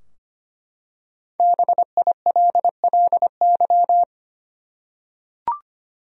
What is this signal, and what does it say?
Morse code, heard as a keyed tone, about 25 wpm, 700 Hz. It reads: BILLY
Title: USA Common Names - 25wpm